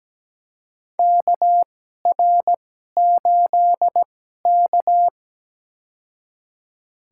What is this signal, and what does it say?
Morse code, heard as a keyed tone, about 17 wpm, spaced as for 12 wpm, 700 Hz. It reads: KR8K
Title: USA Call Signs - 4 Characters - 2x Character Spacing - 17wpm